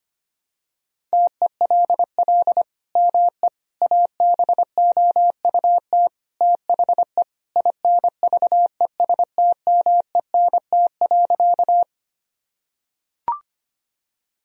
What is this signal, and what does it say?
Morse code, heard as a keyed tone, about 25 wpm, 700 Hz. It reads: TELL ME ABOUT THE INVESTMENT.
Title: Sentences from Top 1000 Words - Review Part 4 - 25wpm